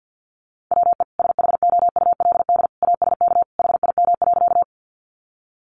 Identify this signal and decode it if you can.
Morse code, heard as a keyed tone, about 50 wpm, 700 Hz. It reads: WE SHOULD ASK HIM.
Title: Sentences from Top 300 Words 50wpm